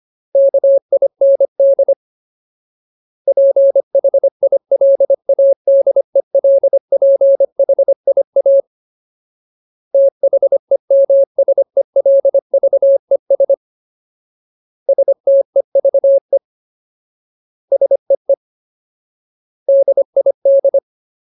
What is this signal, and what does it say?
Morse code, heard as a keyed tone, about 25 wpm, 550 Hz. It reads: KIND PHILADELPHIA THEMSELVES STEVE SEE DID
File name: Ultimate Word List / 25WPM Ultimate